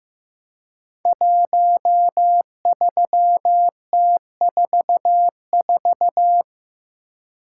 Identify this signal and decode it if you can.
Morse code, heard as a keyed tone, about 15 wpm, 700 Hz. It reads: 13T44
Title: Code Groups - 5 Letter:Numbers - 15wpm